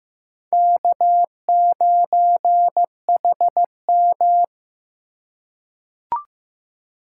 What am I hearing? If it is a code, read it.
Morse code, heard as a keyed tone, about 15 wpm, 700 Hz. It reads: K9HM
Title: USA Call Signs - 4 Characters - 2x Character Spacing - 15wpm